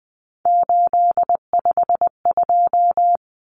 Morse code, heard as a keyed tone, about 20 wpm, 700 Hz. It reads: 852
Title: Common Cut Numbers - 3-Digit 20wpm